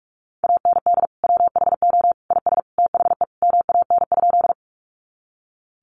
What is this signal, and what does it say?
Morse code, heard as a keyed tone, about 44 wpm, 700 Hz. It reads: AND WHO IS THE MAN?
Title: Sentences from Top 200 Words 45wpm